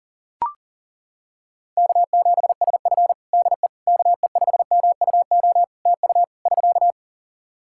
Morse code, heard as a keyed tone, about 40 wpm, 700 Hz. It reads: K7SF DE KE5MUO TU <SK>
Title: K1USN SST QSO Practice 40wpm